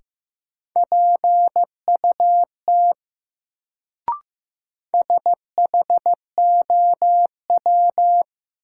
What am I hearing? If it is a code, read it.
Morse code, heard as a keyed tone, about 15 wpm, 700 Hz. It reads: PUT SHOW
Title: Sentences from Top 200 Words - New Word Drill - 15wpm